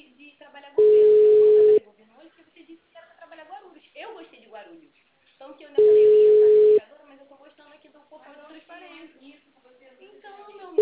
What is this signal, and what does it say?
Ringback tone (Europe, 425 Hz), with speech.
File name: out- -2012- - - .25224